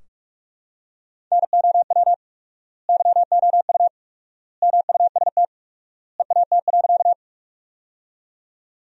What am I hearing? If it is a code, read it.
Morse code, heard as a keyed tone, about 46 wpm, 700 Hz. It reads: NOW YOU MUST EAT.
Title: 4x Word-Spacing - Sentences from Top 300 Words - 45wpm